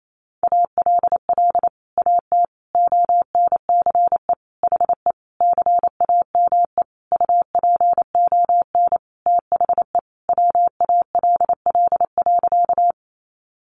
Morse code, heard as a keyed tone, about 28 wpm, 700 Hz. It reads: ALL AT ONCE HE CAME UPON THE WALL.
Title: Sentences from Top 1100 Words 28wpm